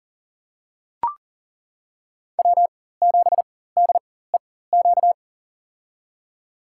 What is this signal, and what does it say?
Morse code, heard as a keyed tone, about 40 wpm, spaced as for 19 wpm, 700 Hz. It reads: W7DEQ
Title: USA Call Signs - 5 Characters - 4x Character Spacing - 40wpm